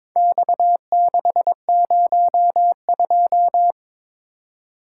Morse code, heard as a keyed tone, about 22 wpm, 700 Hz. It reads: X602
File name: practice / 22wpm / Four Letter-Number Combinations - 22wpm